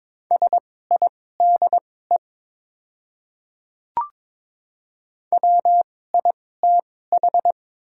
Morse code, heard as a keyed tone, about 21 wpm, spaced as for 16 wpm, 700 Hz. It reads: SIDE WITH
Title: 2x Character Spacing - Top 100 Words - 22wpm